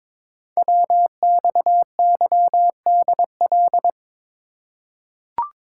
Morse code, heard as a keyed tone, about 22 wpm, 700 Hz. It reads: WXYDL
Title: Code Groups - 5 Letters - 22wpm